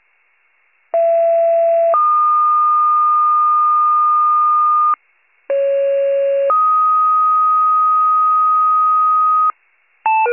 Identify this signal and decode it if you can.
Two-tone page, heard as a one-second tone, then a three-second tone, about 670 then 1220 Hz, about 550 then 1250 Hz. It is heard from 0.9 to 9.5 s.
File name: 28 FIRE